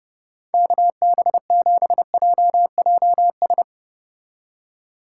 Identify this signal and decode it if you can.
Morse code, heard as a keyed tone, about 37 wpm, 700 Hz. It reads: KB7JJS
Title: Speed Racing - USA 2x3 Call Signs 30wpm